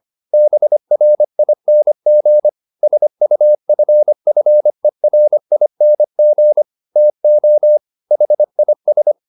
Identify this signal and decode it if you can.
Morse code, heard as a keyed tone, about 25 wpm, 600 Hz. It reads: BRING SUFFERING TO HIS